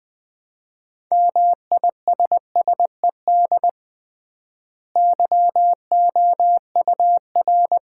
Morse code, heard as a keyed tone, about 20 wpm, 700 Hz. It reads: MISSED YOUR
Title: Common QSO Phrases - 3x Word-Spacing - 20wpm